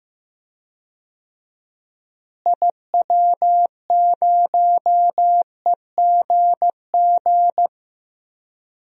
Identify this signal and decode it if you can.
Morse code, heard as a keyed tone, about 15 wpm, 700 Hz. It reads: IW0EGG
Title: DXCC-100 Call Signs - 6 Characters - Repeated 2x - 15wpm